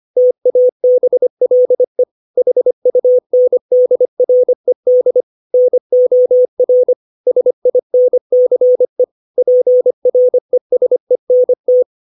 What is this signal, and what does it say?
Morse code, heard as a keyed tone, about 25 wpm, 500 Hz. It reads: TABLE HUNDRED NOR SINCE PRESENT
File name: 632 words / 632 Words 25WPM